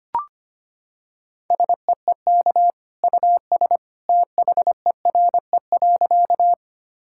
Morse code, heard as a keyed tone, about 25 wpm, 700 Hz. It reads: SEEK US THERE.
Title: Sentences from Top 1600 Words 25wpm